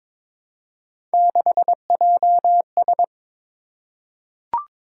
Morse code, heard as a keyed tone, about 22 wpm, 700 Hz. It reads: 6JS
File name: practice / 22wpm / Three Letter-Number Combinations - 22wpm